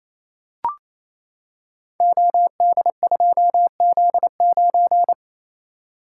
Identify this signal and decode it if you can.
Morse code, heard as a keyed tone, about 28 wpm, 700 Hz. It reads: OD2Z9
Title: Code Groups - 5 Letter:Numbers - 28wpm